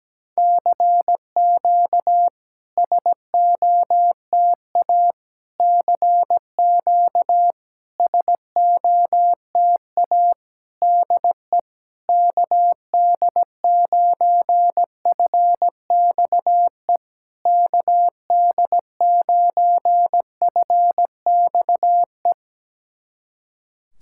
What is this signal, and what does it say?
Morse code, heard as a keyed tone, about 17 wpm, 700 Hz. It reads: CQ SOTA CQ SOTA DE KD9FXE KD9FXE
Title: SOTA QSO Practice 17wpm